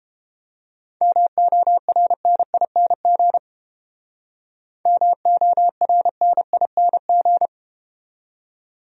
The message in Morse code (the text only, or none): MORNING MORNING